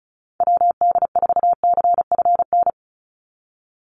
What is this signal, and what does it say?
Morse code, heard as a keyed tone, about 34 wpm, 700 Hz. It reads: WD4CFN
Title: NA SOTA Chaser Callsigns 35wpm